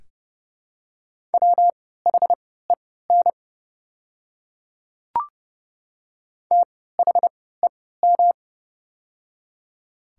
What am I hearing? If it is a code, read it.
Morse code, heard as a keyed tone, about 30 wpm, spaced as for 17 wpm, 700 Hz. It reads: WHEN THEM
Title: 3x Character Spacing - Top 100 Words - 30wpm